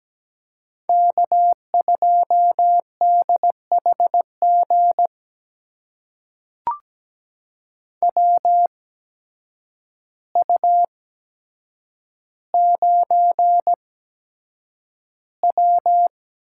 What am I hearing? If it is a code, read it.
Morse code, heard as a keyed tone, about 17 wpm, spaced as for 8 wpm, 700 Hz. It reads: K2DHG WU9W
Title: USA Call Signs - 8x Character Spacing - 17wpm